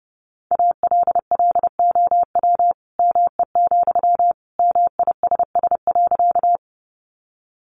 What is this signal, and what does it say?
Morse code, heard as a keyed tone, about 30 wpm, 700 Hz. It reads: ALLOW ME, MISS.
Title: Sentences from Top 700 Words 30wpm